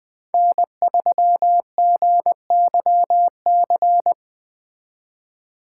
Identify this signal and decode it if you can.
Morse code, heard as a keyed tone, about 20 wpm, 700 Hz. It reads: N3GYC